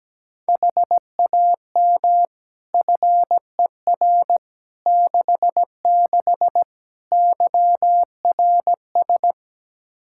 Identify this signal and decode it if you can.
Morse code, heard as a keyed tone, about 17 wpm, 700 Hz. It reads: HAM FER 66 YRS
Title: Practice QSO - Exchange 3 - More Info - 17wpm